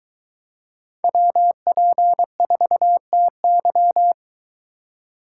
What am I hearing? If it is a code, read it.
Morse code, heard as a keyed tone, about 23 wpm, 700 Hz. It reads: WP4TY